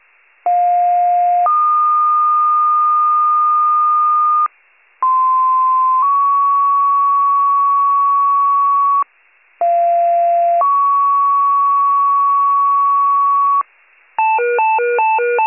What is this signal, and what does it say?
Two-tone page, heard as a one-second tone, then a three-second tone, about 710 then 1220 Hz, about 1040 then 1120 Hz, about 690 then 1120 Hz. It is heard from 0.5 to 13.6 s.